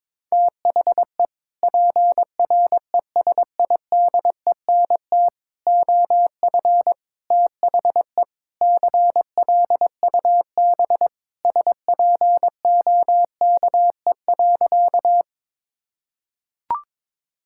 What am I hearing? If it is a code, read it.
Morse code, heard as a keyed tone, about 21 wpm, 700 Hz. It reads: THE PRESIDENT OF THE CLUB SPOKE.
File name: practice / 22wpm / Sentences with 4-Word Phrases - 22wpm